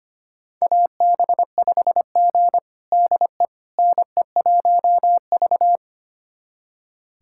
Morse code, heard as a keyed tone, about 25 wpm, 700 Hz. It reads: AB5G DE NE1V
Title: USA Callsign DE Callsign - 4 Characters - 25wpm